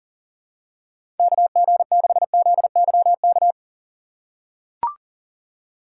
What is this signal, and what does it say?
Morse code, heard as a keyed tone, about 40 wpm, 700 Hz. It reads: KG6ZYK
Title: USA Call Signs - 6 Characters - 3x Character Spacing - 40wpm